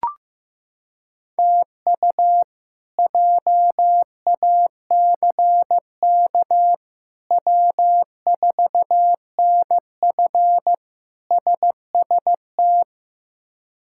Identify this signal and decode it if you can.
Morse code, heard as a keyed tone, about 15 wpm, 700 Hz. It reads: TU JACK W4NF SST
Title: K1USN SST QSO Practice - 15wpm